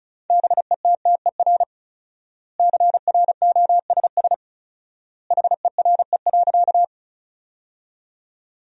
Morse code, heard as a keyed tone, about 34 wpm, 700 Hz. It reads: BETTER CROSS HERE.
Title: 4x Word-Spacing - Sentences from Top 400 Words - 35wpm